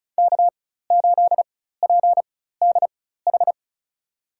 Morse code, heard as a keyed tone, about 34 wpm, spaced as for 16 wpm, 700 Hz. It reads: K8PDH
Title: USA Call Signs - 5 Characters - 4x Character Spacing - 35wpm